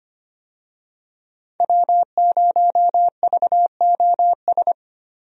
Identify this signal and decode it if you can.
Morse code, heard as a keyed tone, about 25 wpm, 700 Hz. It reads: W0VOS
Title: USA Call Signs - 5 Characters - Repeated 3x - 25wpm